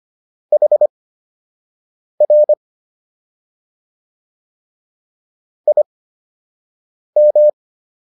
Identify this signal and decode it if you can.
Morse code, heard as a keyed tone, about 25 wpm, spaced as for 6 wpm, 600 Hz. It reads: HR IM